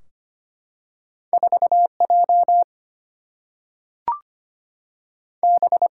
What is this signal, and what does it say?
Morse code, heard as a keyed tone, about 25 wpm, spaced as for 3 wpm, 700 Hz. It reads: 4 JB
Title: Single-Letter Single-Number 25wpm